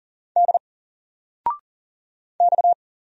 Morse code, heard as a keyed tone, about 40 wpm, 700 Hz. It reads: D X
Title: Single Letter-Number 40wpm